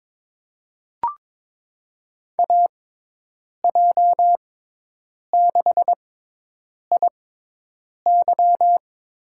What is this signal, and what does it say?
Morse code, heard as a keyed tone, about 22 wpm, spaced as for 8 wpm, 700 Hz. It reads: AJ6IY
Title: USA Call Signs - 6x Character Spacing - 22wpm